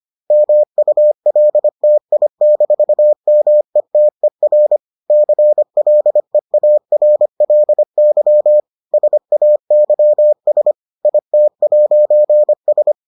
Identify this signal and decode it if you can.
Morse code, heard as a keyed tone, about 25 wpm, 600 Hz. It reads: MULTI-METER CLEARLY SAYS IT'S